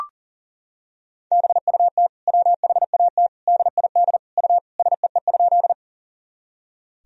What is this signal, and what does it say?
Morse code, heard as a keyed tone, about 40 wpm, 700 Hz. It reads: BUT WHAT DID U SEE?
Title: Sentences from Top 100 Words and Abbreviations 40wpm